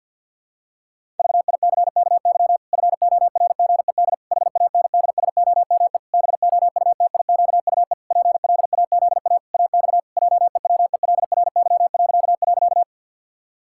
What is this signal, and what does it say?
Morse code, heard as a keyed tone, about 51 wpm, 700 Hz. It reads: VICKY FORGED HANDSOME BOUTIQUE PLAZA AX JEWELRY..
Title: Pangram Challenge v2 - 50wpm